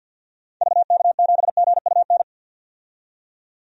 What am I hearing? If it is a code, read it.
Morse code, heard as a keyed tone, about 52 wpm, 700 Hz. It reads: VK7GUN
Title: DXCC-100 Call Signs - 6 Characters - 50wpm